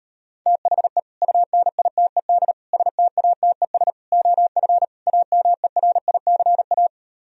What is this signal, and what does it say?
Morse code, heard as a keyed tone, about 32 wpm, 700 Hz. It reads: THE UNITED STATES OF AMERICA
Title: Speed-Racing - Sets of 5 Words 28wpm